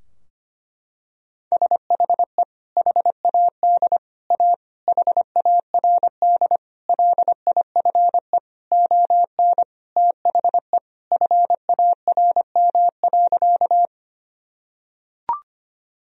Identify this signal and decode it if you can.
Morse code, heard as a keyed tone, about 25 wpm, 700 Hz. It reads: SHE HAD A HARD LIFE ON THE FARM.
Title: Sentences from Top 1000 Words - Review Part 2 - 25wpm